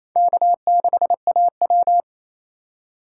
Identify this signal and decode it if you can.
Morse code, heard as a keyed tone, about 28 wpm, 700 Hz. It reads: K6AW
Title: USA Call Signs - 4x Character Spacing - 28wpm